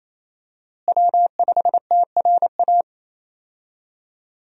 Morse code, heard as a keyed tone, about 28 wpm, 700 Hz. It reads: W5TRA
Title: USA Call Signs - 5 Characters - Repeated 2x - 28wpm